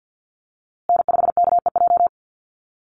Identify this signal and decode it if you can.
Morse code, heard as a keyed tone, about 48 wpm, 700 Hz. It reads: N5KEJ
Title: USA Call Signs Repeated 2x 50wpm